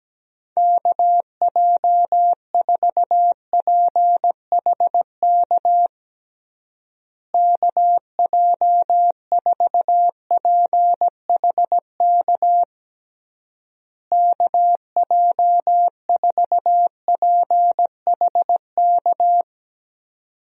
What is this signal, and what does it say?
Morse code, heard as a keyed tone, about 17 wpm, 700 Hz. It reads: KJ4PHK KJ4PHK KJ4PHK
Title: USA Call Signs - 6 Characters - Repeated 3x - 17wpm